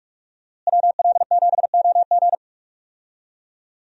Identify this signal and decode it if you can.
Morse code, heard as a keyed tone, about 43 wpm, 700 Hz. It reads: WL7OG
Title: USA Call Signs - 5 Characters - Repeated 2x - 45wpm